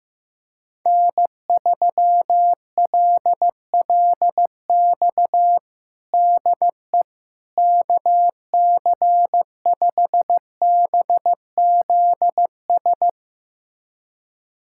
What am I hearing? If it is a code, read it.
Morse code, heard as a keyed tone, about 15 wpm, 700 Hz. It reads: N3LLX DE KC5BZS
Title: USA Callsign DE Callsign 15wpm